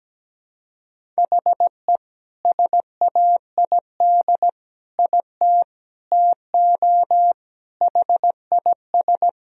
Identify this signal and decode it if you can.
Morse code, heard as a keyed tone, about 17 wpm, 700 Hz. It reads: HE SAID IT TO HIS
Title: Sentences from Top 1000 Words - Review Part 1 - 17wpm